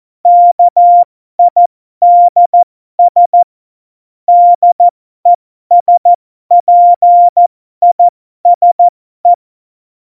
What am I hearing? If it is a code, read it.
Morse code, heard as a keyed tone, about 14 wpm, spaced as for 12 wpm, 700 Hz. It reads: KIDS DESPISE
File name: Morsekurs / Witze